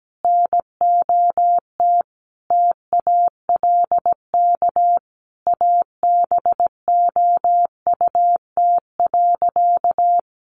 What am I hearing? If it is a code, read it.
Morse code, heard as a keyed tone, about 17 wpm, 700 Hz. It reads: NOT TALK ABOUT.
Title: Sentences from Top 800 Words 17wpm